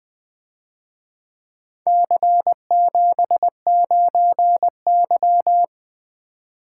Morse code, heard as a keyed tone, about 23 wpm, 700 Hz. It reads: C79Y
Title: Speed-Racing - Four Letter-Number Combinations - 20wpm